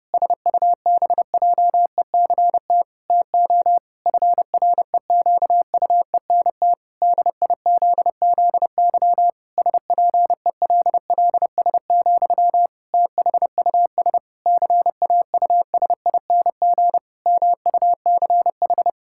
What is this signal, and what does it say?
Morse code, heard as a keyed tone, about 32 wpm, 700 Hz. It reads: SUBJECT TO FREQUENT DIZZY SPELLS, THUS CAUSING MUCH
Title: Pangram Challenge v2 - 30wpm